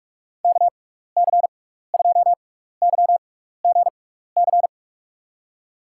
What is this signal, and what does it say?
Morse code, heard as a keyed tone, about 41 wpm, spaced as for 15 wpm, 700 Hz. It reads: KC2YGC
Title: USA Call Signs - 6 Characters - 6x Character Spacing - 45wpm